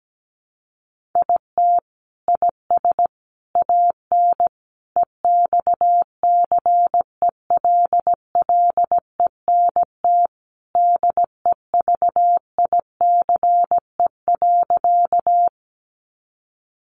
Morse code, heard as a keyed tone, about 17 wpm, 700 Hz. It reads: IT IS AN EXCELLENT DEVICE.
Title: Sentences from Top 1500 Words 17wpm